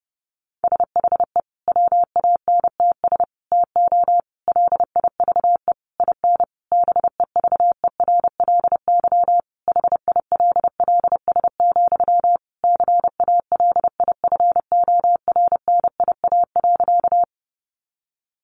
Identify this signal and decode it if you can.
Morse code, heard as a keyed tone, about 30 wpm, 700 Hz. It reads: SHE WANTS TO LIVE IN BEVERLY HILLS, CALIFORNIA.
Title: Easy Phrases 30wpm